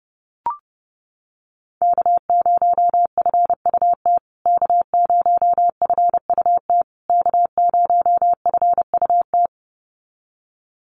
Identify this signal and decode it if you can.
Morse code, heard as a keyed tone, about 30 wpm, 700 Hz. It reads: K0FUT K0FUT K0FUT
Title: USA Call Signs Repeated 3x 30wpm